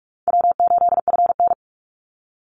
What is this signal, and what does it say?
Morse code, heard as a keyed tone, about 44 wpm, 700 Hz. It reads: W8FN
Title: Easy Call Signs Repeated 3x - 45wpm